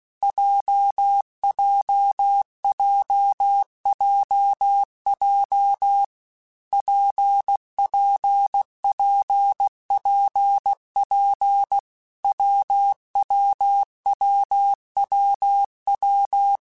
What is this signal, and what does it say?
Morse code, heard as a keyed tone, about 16 wpm, 800 Hz. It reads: JJJJJ PPPPP WWWWW
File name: Difficult Combos / Difficult Combos-mp3s / WPJ WPJ1 @15